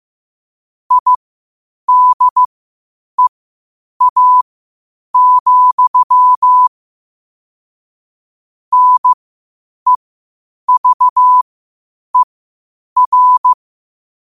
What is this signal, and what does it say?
Morse code, heard as a keyed tone, about 15 wpm, spaced as for 8 wpm, 1000 Hz. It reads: IDEA, NEVER
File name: qotdfiles / 8wpm